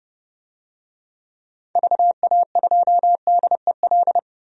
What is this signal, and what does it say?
Morse code, heard as a keyed tone, about 30 wpm, 700 Hz. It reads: VA2DEL
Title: DXCC-100 Call Signs - 6 Characters - 30wpm